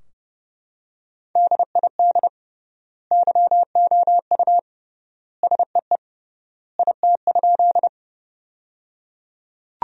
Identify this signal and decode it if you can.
Morse code, heard as a keyed tone, about 30 wpm, 700 Hz. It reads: DID YOU SEE IT?
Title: 3x Word-Spacing - Sentences from Top 100 Words - 30wpm